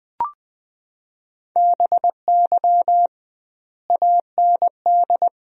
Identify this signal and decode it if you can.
Morse code, heard as a keyed tone, about 20 wpm, 700 Hz. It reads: BY AND
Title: 2x Word-Spacing - Sentences from Top 300 Words - 20wpm